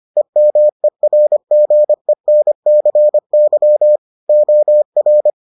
Morse code, heard as a keyed tone, about 25 wpm, 600 Hz. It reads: EMERGENCY OR